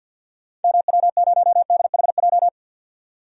Morse code, heard as a keyed tone, about 52 wpm, 700 Hz. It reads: MW0DHJ
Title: DXCC-100 Call Signs - 6 Characters - 50wpm